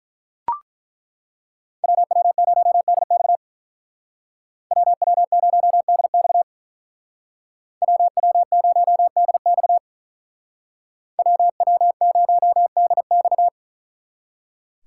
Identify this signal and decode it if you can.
Morse code, heard as a keyed tone, about 43 wpm, 700 Hz. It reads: WW0DX WW0DX WW0DX WW0DX
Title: Speed Racing - USA 2x2 Call Signs 35wpm